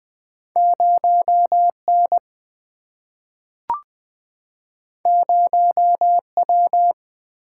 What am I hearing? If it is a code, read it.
Morse code, heard as a keyed tone, about 20 wpm, 700 Hz. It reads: 0 N 0 W
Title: Single-Letter Single-Number 20wpm